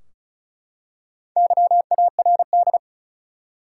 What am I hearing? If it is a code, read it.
Morse code, heard as a keyed tone, about 35 wpm, 700 Hz. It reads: YARD
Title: Four Letter Words - 35wpm